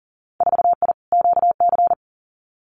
Morse code, heard as a keyed tone, about 38 wpm, 700 Hz. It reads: 4I QC
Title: Winter Field Day Exchanges Encore 40wpm